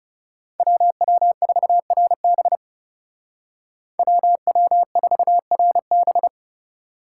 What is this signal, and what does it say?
Morse code, heard as a keyed tone, about 37 wpm, 700 Hz. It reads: WW4RB WW4RB
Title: Speed Racing - USA 2x2 Call Signs 30wpm